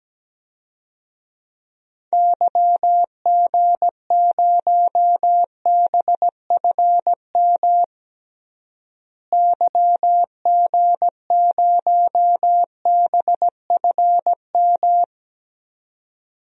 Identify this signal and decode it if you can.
Morse code, heard as a keyed tone, about 17 wpm, 700 Hz. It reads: YG0BFM YG0BFM
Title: DXCC-100 Call Signs - 6 Characters - Repeated 2x - 17wpm